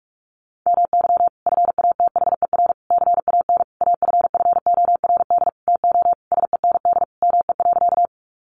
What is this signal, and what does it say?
Morse code, heard as a keyed tone, about 44 wpm, 700 Hz. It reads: MY FATHER CAN AFFORD TO SEND ME.
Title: Sentences from Top 1500 Words 45wpm